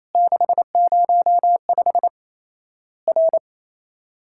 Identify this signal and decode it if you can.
Morse code, heard as a keyed tone, about 28 wpm, 700 Hz. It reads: 605 R
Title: Worked All Europe DX Contest - QTC Exchange - 28wpm